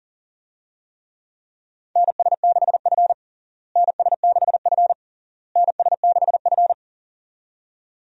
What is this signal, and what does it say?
Morse code, heard as a keyed tone, about 40 wpm, 700 Hz. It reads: NS6F NS6F NS6F
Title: USA Call Signs - 4 Characters - Repeated 3x - 40wpm